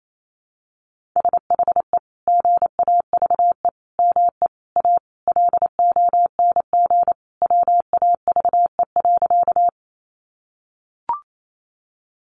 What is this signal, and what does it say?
Morse code, heard as a keyed tone, about 28 wpm, 700 Hz. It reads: SHE GAVE ME A LONG WAVE.
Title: Sentences from Top 1400 Words 28wpm